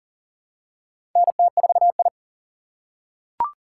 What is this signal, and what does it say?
Morse code, heard as a keyed tone, about 40 wpm, 700 Hz. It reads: NT4I
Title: USA Call Signs - 6x Character Spacing - 40wpm